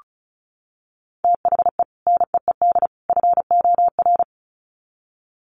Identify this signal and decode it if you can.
Morse code, heard as a keyed tone, about 34 wpm, 700 Hz. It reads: THE NEED FOR
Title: Sets of 3 Words 35wpm